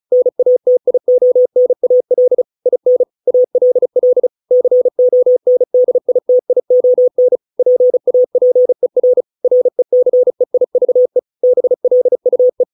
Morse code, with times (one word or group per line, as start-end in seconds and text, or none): 0.1-2.4 NATIONAL
2.7-3.0 IN
3.3-4.3 ALL
4.5-7.4 CONDITION
7.6-9.2 PAPER
9.4-11.2 RECEIVE
11.4-12.6 BLUE